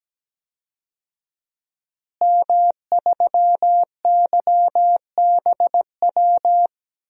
Morse code, heard as a keyed tone, about 17 wpm, 700 Hz. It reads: M3YBW
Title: DXCC-100 Call Signs - 5 Characters - Repeated 3x - 17wpm